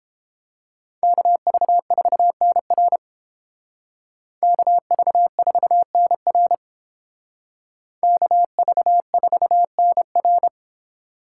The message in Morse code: KV4NR KV4NR KV4NR

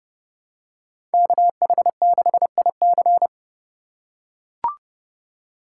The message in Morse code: KH6IC

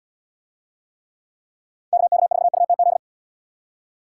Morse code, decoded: KK4UTZ

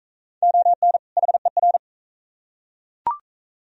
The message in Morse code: ON HER

